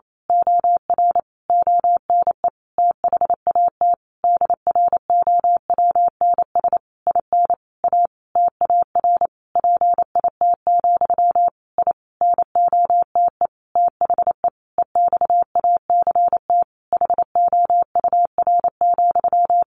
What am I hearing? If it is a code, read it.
Morse code, heard as a keyed tone, about 28 wpm, 700 Hz. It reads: OR ONE THAT DROWNS IN A TAR PIT, I NOTE THE EXACT HOUR,
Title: A Sound Of Thunder 28wpm